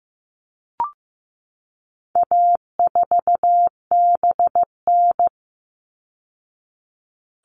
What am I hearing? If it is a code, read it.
Morse code, heard as a keyed tone, about 15 wpm, 700 Hz. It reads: A4BN
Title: All Cut Numbers - 4-Digit 15wpm